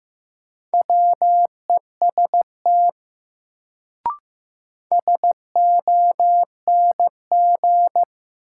Morse code, heard as a keyed tone, about 15 wpm, 700 Hz. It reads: WEST SONG
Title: Sentences from Top 400 Words - New Word Drill - 15wpm